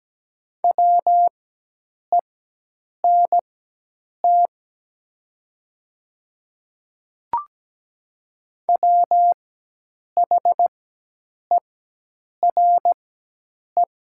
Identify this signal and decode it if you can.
Morse code, heard as a keyed tone, about 17 wpm, spaced as for 8 wpm, 700 Hz. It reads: WENT WHERE
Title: 4x Character Spacing - Top 200 Words - 17wpm